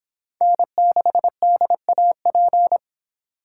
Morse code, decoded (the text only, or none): N6DAP